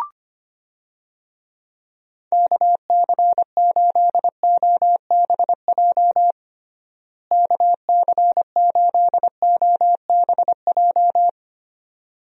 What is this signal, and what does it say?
Morse code, heard as a keyed tone, about 25 wpm, 700 Hz. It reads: KC8OBJ KC8OBJ
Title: USA Call Signs - 6 Characters - Repeated 2x - 25wpm